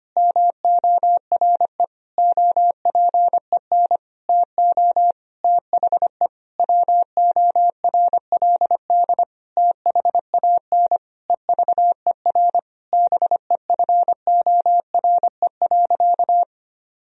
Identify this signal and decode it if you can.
Morse code, heard as a keyed tone, about 25 wpm, 700 Hz. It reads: MORE OPEN TO THE WORLD THAN EVER BEFORE.
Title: Sentences from Top 1000 Words - Review Part 3 - 25wpm